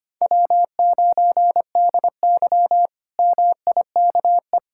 Morse code, heard as a keyed tone, about 25 wpm, 700 Hz. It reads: W9DY MIKE
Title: K1USN SST QSO Practice - 25wpm